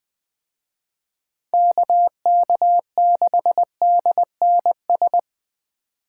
Morse code, heard as a keyed tone, about 20 wpm, 700 Hz. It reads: KK6DNS
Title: Speed Racing - USA 2x3 Call Signs 17wpm